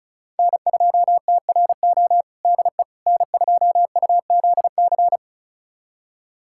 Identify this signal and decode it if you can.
Morse code, heard as a keyed tone, about 34 wpm, 700 Hz. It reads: N2TRO DE N2UZC
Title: USA Callsign DE Callsign - 5 Characters - 35wpm